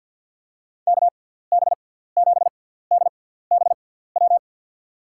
Morse code, heard as a keyed tone, about 48 wpm, spaced as for 17 wpm, 700 Hz. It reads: KB7DBW
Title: USA Call Signs - 6 Characters - 6x Character Spacing - 50wpm